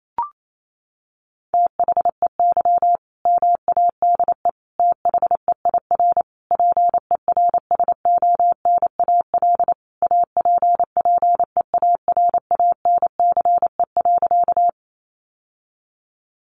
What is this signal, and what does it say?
Morse code, heard as a keyed tone, about 28 wpm, 700 Hz. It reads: THEY MADE THEIR PERSONAL APPEARANCE.
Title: Sentences from Top 1100 Words 28wpm